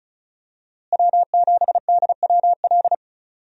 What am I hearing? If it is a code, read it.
Morse code, heard as a keyed tone, about 34 wpm, 700 Hz. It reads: W7DWL
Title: USA Call Signs - 5 Characters - Repeated 2x - 35wpm